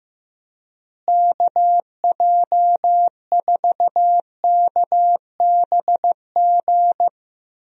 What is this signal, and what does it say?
Morse code, heard as a keyed tone, about 15 wpm, 700 Hz. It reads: KJ4KBG
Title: USA Call Signs - 6 Characters - Repeated 2x - 15wpm